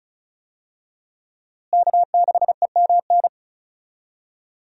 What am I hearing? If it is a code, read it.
Morse code, heard as a keyed tone, about 36 wpm, 700 Hz. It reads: K6EMN
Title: USA Call Signs - 5 Characters - Repeated 2x - 35wpm